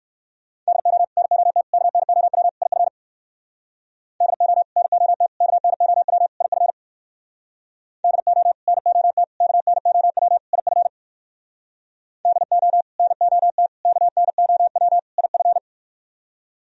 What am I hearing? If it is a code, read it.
Morse code, heard as a keyed tone, about 69 wpm, 700 Hz. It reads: DO NOT KNOW IF DO NOT KNOW IF DO NOT KNOW IF DO NOT KNOW IF